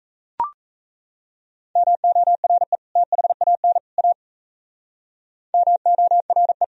Morse code, heard as a keyed tone, about 35 wpm, 700 Hz. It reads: MORE THAN A MORE